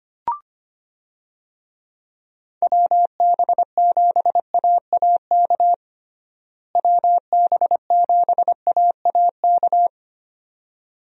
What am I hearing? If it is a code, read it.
Morse code, heard as a keyed tone, about 25 wpm, 700 Hz. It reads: WB7AAK WB7AAK
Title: USA Call Signs - 6 Characters - Repeated 2x - 25wpm